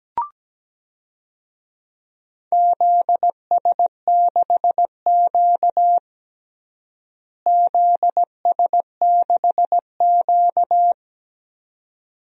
Morse code, heard as a keyed tone, about 17 wpm, 700 Hz. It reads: ZS6Q ZS6Q
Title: DXCC-100 Call Signs - 4 Characters - Repeated 2x - 17wpm